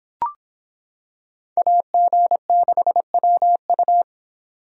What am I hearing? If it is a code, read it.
Morse code, heard as a keyed tone, about 26 wpm, 700 Hz. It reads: AG6WU